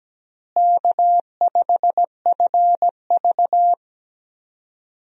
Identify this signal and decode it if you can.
Morse code, heard as a keyed tone, about 17 wpm, 700 Hz. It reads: K5FV
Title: USA Call Signs - 4 Characters - Repeated 3x - 17wpm